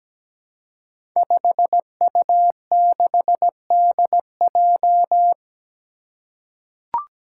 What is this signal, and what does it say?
Morse code, heard as a keyed tone, about 17 wpm, 700 Hz. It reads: 5U6DJ
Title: Five Challenging Letter-Number Combinations 17wpm